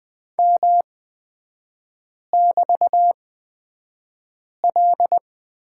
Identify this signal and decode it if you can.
Morse code, heard as a keyed tone, about 20 wpm, spaced as for 5 wpm, 690 Hz. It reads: M<BT>L
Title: 229 Lesson 37e - Letter x - Three Characters 20wpm